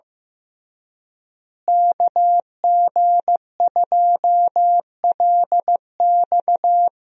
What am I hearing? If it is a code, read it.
Morse code, heard as a keyed tone, about 15 wpm, 700 Hz. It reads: KG2LX
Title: USA Call Signs - 5 Characters - Repeated 3x - 15wpm